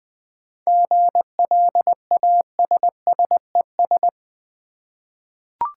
Morse code, heard as a keyed tone, about 20 wpm, 700 Hz. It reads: GLASSES
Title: Easy Words with Hints - 20wpm